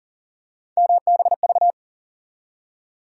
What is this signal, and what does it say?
Morse code, heard as a keyed tone, about 40 wpm, 700 Hz. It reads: MBV
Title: USA Call Sign Postfixes - 3 Letters 40wpm